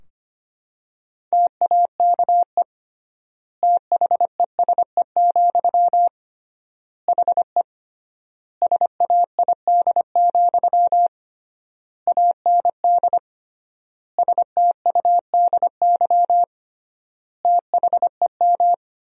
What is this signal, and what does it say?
Morse code, heard as a keyed tone, about 25 wpm, 700 Hz. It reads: TAKE THESE, HE SAID, AND STUDY THEM
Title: 3x Word-Spacing - Sentences from Top 300 Words - 25wpm